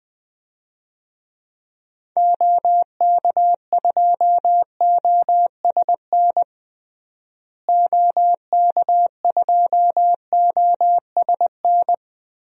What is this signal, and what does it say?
Morse code, heard as a keyed tone, about 20 wpm, 700 Hz. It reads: OK2OSN OK2OSN
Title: DXCC-100 Call Signs - 6 Characters - Repeated 3x - 20wpm